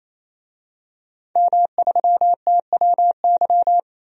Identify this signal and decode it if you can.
Morse code, heard as a keyed tone, about 28 wpm, 700 Hz. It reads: M3TWY
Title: DXCC-100 Call Signs - 5 Characters - 28wpm